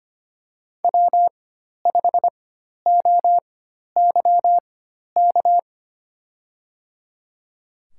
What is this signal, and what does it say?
Morse code, heard as a keyed tone, about 25 wpm, spaced as for 12 wpm, 690 Hz. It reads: W5OYK
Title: USA Call Signs - 4x Character Spacing - 25wpm